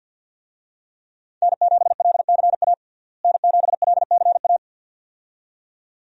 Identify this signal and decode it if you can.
Morse code, heard as a keyed tone, about 52 wpm, 700 Hz. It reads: N7LCA N7LCA
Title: USA Call Signs - 5 Characters - Repeated 2x - 50wpm